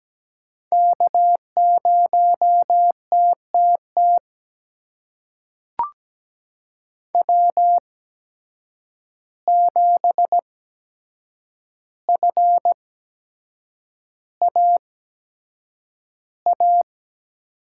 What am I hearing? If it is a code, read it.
Morse code, heard as a keyed tone, about 17 wpm, spaced as for 7 wpm, 700 Hz. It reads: K0TTT W7FAA
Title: USA Call Signs - 5 Characters - 8x Character Spacing - 17wpm